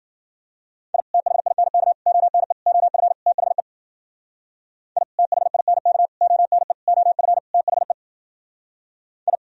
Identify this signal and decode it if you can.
Morse code, heard as a keyed tone, about 52 wpm, 700 Hz. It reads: I THINK ONE OF THE I THINK ONE OF THE I